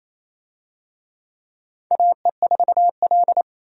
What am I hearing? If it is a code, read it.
Morse code, heard as a keyed tone, about 28 wpm, 700 Hz. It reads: AE4L
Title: USA Call Signs - 4 Characters - 28wpm